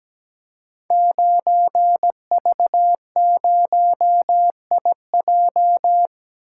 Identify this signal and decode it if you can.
Morse code, heard as a keyed tone, about 17 wpm, 700 Hz. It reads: 9V0IJ
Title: Code Groups - 5 Letter:Numbers - 17wpm